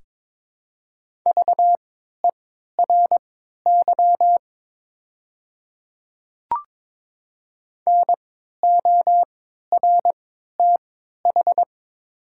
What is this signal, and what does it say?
Morse code, heard as a keyed tone, about 22 wpm, spaced as for 13 wpm, 700 Hz. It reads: VERY NORTH
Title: 3x Character Spacing - Top 300 Words - 22wpm